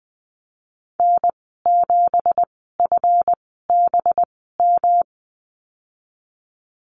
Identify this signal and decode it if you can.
Morse code, heard as a keyed tone, about 20 wpm, spaced as for 15 wpm, 700 Hz. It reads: N7FBM
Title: 186 Lesson 30g - Letter b - Call Signs 20wpm